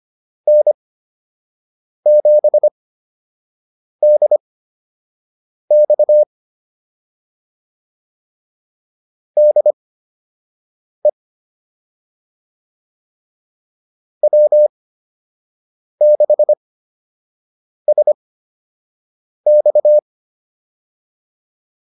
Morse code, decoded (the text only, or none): N7DX DE W6SX